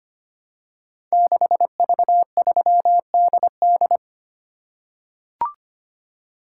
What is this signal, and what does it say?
Morse code, heard as a keyed tone, about 25 wpm, 700 Hz. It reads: 6V3DD
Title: Code Groups - 5 Letter:Numbers - 25wpm